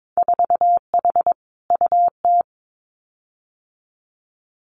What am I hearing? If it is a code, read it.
Morse code, heard as a keyed tone, about 22 wpm, 700 Hz. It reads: 4H UT
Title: Winter Field Day Exchanges Encore 22wpm